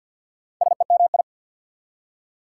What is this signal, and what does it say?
Morse code, heard as a keyed tone, about 53 wpm, 690 Hz. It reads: SEMI
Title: Prefixes and Suffixes 50wpm